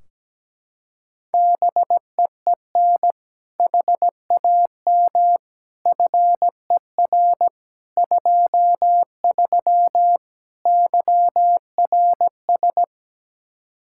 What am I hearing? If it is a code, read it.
Morse code, heard as a keyed tone, about 17 wpm, 700 Hz. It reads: BEEN HAM FER 23 YRS
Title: Practice QSO - Exchange 3 - More Info - 17wpm